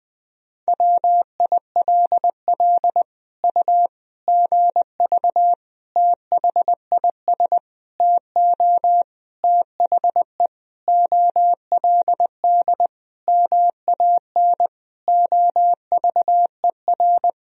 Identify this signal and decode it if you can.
Morse code, heard as a keyed tone, about 20 wpm, 700 Hz. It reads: WILL U GV THIS TO THE OLD MAN OVER
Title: Sentences from Top 200 Words and Abbreviations 20wpm